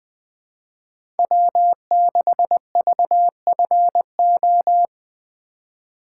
Morse code, heard as a keyed tone, about 20 wpm, 700 Hz. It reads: W6VFO
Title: USA Call Signs - 5 Characters - 4x Character Spacing - 20wpm